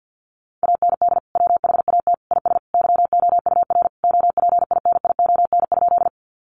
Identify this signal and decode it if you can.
Morse code, heard as a keyed tone, about 50 wpm, 700 Hz. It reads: AND WHAT IS YOUR OPINION?
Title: Sentences from Top 1200 Words 50wpm